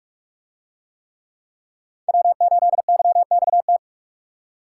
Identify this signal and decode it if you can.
Morse code, heard as a keyed tone, about 46 wpm, 700 Hz. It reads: W8YXT